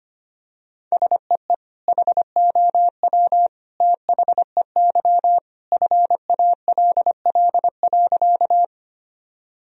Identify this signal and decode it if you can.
Morse code, heard as a keyed tone, about 25 wpm, 700 Hz. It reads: SEE HOW THEY FALL.
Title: Sentences from Top 1000 Words - Review Part 2 - 25wpm